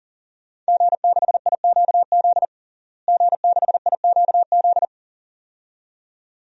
Morse code, heard as a keyed tone, about 40 wpm, 700 Hz. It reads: G6IQZ G6IQZ
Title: DXCC-100 Call Signs - 5 Characters - Repeated 2x - 40wpm